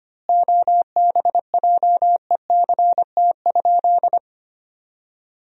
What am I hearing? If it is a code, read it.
Morse code, heard as a keyed tone, about 25 wpm, 700 Hz. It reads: OBJECT?
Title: 3x Word-Spacing - Sentences from Top 500 Words - 25wpm